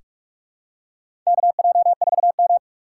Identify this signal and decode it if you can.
Morse code, heard as a keyed tone, about 46 wpm, 700 Hz. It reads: KJ4M
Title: USA Call Signs - 4 Characters - 4x Character Spacing - 45wpm